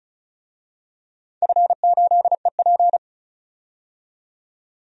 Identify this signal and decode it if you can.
Morse code, heard as a keyed tone, about 36 wpm, 690 Hz. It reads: F8EP
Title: DXCC-100 Call Signs - 4 Characters - 35wpm